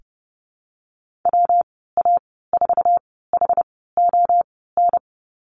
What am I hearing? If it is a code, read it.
Morse code, heard as a keyed tone, about 30 wpm, spaced as for 17 wpm, 700 Hz. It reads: WA4HON
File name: Lesson 19g - Letter w - Call Signs 30wpm